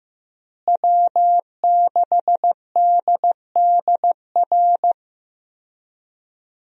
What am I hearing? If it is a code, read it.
Morse code, heard as a keyed tone, about 16 wpm, 700 Hz. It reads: W6DDR
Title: USA Call Signs - 5 Characters - Repeated 3x - 15wpm